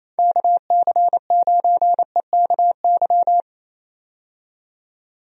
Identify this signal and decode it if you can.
Morse code, heard as a keyed tone, about 28 wpm, 700 Hz. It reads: KC9EKY
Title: USA Call Signs - 6 Characters - 28wpm